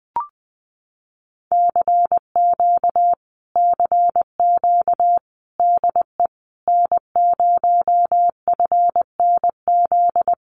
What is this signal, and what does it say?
Morse code, heard as a keyed tone, about 20 wpm, 700 Hz. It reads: CQ CQ DE N0FNZ
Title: Common QSO Phrases Encore 20wpm